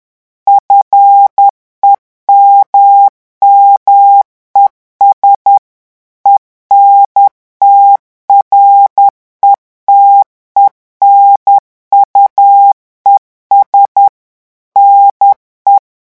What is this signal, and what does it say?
Morse code, heard as a keyed tone, about 10 wpm, 800 Hz. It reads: FEMMES ENTRETENUES NE
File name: morse mp3 / revision9 citations humoristiques vitesse 10 mots minute